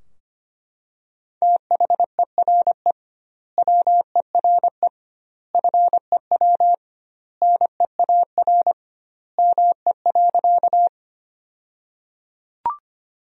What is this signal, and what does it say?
Morse code, heard as a keyed tone, about 25 wpm, 700 Hz. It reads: THERE WERE FEW NEAR ME.
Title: 2x Word-Spacing - Sentences from Top 300 Words - 25wpm